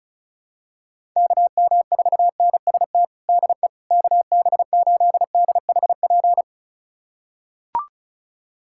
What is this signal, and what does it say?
Morse code, heard as a keyed tone, about 34 wpm, 700 Hz. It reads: KM4NST DE KB8DHP
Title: USA Callsign DE Callsign 35wpm